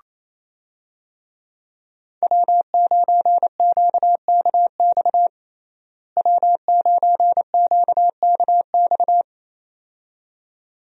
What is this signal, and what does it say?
Morse code, heard as a keyed tone, about 27 wpm, 700 Hz. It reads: W9QKX W9QKX
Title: USA Call Signs - 5 Characters - Repeated 2x - 28wpm